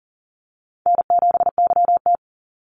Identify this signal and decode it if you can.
Morse code, heard as a keyed tone, about 40 wpm, 700 Hz. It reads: N7YT
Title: Call Signs Repeated 2x - 40wpm